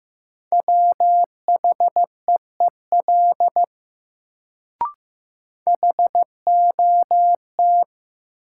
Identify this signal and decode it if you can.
Morse code, heard as a keyed tone, about 15 wpm, 700 Hz. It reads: WHEEL HOT
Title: Sentences from Top 500 Words - New Word Drill - 15wpm